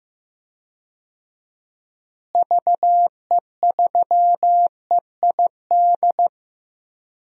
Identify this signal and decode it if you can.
Morse code, heard as a keyed tone, about 15 wpm, 700 Hz. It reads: VE3EID
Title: DXCC-100 Call Signs - 6 Characters - Repeated 3x - 15wpm